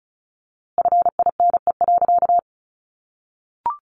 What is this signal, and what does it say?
Morse code, heard as a keyed tone, about 34 wpm, 700 Hz. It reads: FINE.
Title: A Sound Of Thunder 35wpm